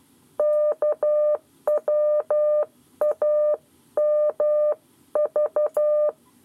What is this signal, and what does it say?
Morse code, heard as a keyed tone, about 12 wpm, 600 Hz. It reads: KWAMV
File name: 7 Lezione cw 1